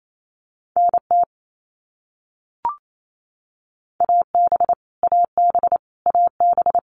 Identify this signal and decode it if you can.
Morse code, heard as a keyed tone, about 28 wpm, 700 Hz. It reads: NT AB AB AB
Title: Canadian Provinces and Territories - Repeated 3x - 28wpm